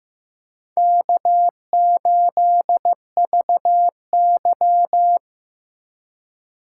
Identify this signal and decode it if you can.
Morse code, heard as a keyed tone, about 15 wpm, 700 Hz. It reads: K8VY
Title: USA Call Signs - 4 Characters - 3x Character Spacing - 15wpm